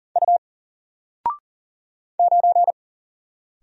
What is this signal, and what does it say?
Morse code, heard as a keyed tone, about 40 wpm, 700 Hz. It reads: U 9
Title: Single Letter-Number Alt 40wpm